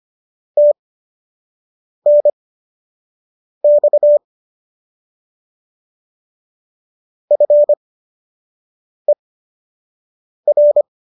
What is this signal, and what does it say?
Morse code, heard as a keyed tone, about 25 wpm, spaced as for 6 wpm, 590 Hz. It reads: TNX FER